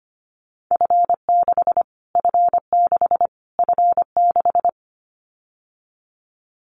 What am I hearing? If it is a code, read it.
Morse code, heard as a keyed tone, about 25 wpm, 700 Hz. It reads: F6 F6 F6
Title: Callsign Prefixes for Top 25 DXCC - Repeated 3x - 25wpm